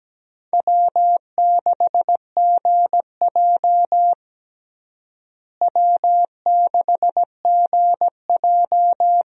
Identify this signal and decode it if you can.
Morse code, heard as a keyed tone, about 17 wpm, 700 Hz. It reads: W6GJ W6GJ
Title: USA Call Signs - 4 Characters - Repeated 2x - 17wpm